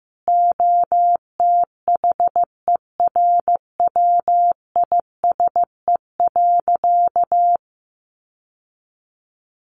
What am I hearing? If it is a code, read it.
Morse code, heard as a keyed tone, about 15 wpm, 700 Hz. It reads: OTHERWISE.
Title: Sentences from Top 1200 Words 15wpm